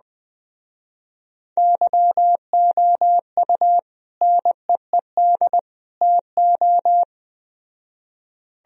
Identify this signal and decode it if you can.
Morse code, heard as a keyed tone, about 20 wpm, 700 Hz. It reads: YOU NEED TO